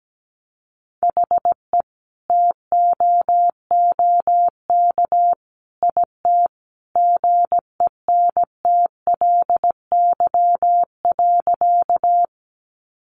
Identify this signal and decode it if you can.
Morse code, heard as a keyed tone, about 17 wpm, 700 Hz. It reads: HE TOOK IT GENTLY.
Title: Sentences from Top 1500 Words 17wpm